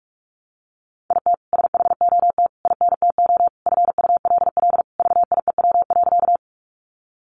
Morse code, heard as a keyed tone, about 44 wpm, 700 Hz. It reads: IT SHOT INTO FULL VIEW.
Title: Sentences from Top 1100 Words 45wpm